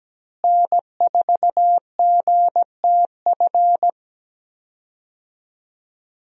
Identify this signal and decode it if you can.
Morse code, heard as a keyed tone, about 17 wpm, 700 Hz. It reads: N4GTF
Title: USA Call Signs - 5 Characters - Repeated 2x - 17wpm